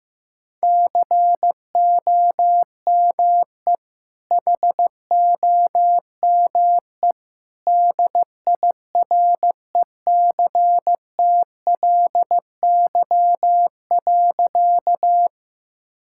Morse code, heard as a keyed tone, about 16 wpm, 700 Hz. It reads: COME HOME DIRECTLY.
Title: Sentences from Top 1000 Words - Review Part 5 - 15wpm